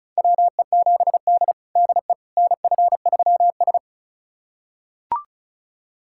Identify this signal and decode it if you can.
Morse code, heard as a keyed tone, about 36 wpm, 700 Hz. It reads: WE7D DE NF3S
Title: USA Callsign DE Callsign - 4 Characters - 35wpm